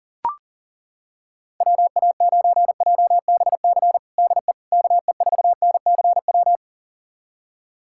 Morse code, heard as a keyed tone, about 37 wpm, 700 Hz. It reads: WA9JBC DE KE4NCW
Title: USA Callsign DE Callsign - 6 Characters - 40wpm